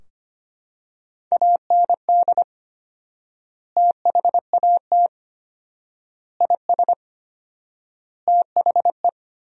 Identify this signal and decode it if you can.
Morse code, heard as a keyed tone, about 25 wpm, 700 Hz. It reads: AND THAT IS THE
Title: 4x Word-Spacing - Sentences from Top 400 Words - 25wpm